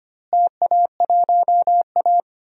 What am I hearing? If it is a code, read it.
Morse code, heard as a keyed tone, about 25 wpm, 700 Hz. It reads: TA1A
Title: DXCC-100 Call Signs - 4 Characters - Repeated 3x - 25wpm